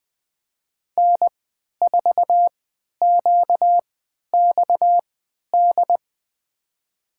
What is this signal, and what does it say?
Morse code, heard as a keyed tone, about 21 wpm, spaced as for 11 wpm, 700 Hz. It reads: N4QXD
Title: USA Call Signs - 5 Characters - 3x Character Spacing - 20wpm